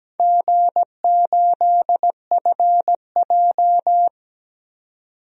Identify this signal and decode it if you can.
Morse code, heard as a keyed tone, about 17 wpm, 700 Hz. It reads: G8FJ
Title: DXCC-100 Call Signs - 4 Characters - 17wpm